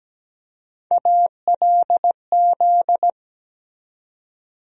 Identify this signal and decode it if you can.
Morse code, heard as a keyed tone, about 17 wpm, 700 Hz. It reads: ALZ
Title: USA Call Sign Postfixes - 3 Letters 17wpm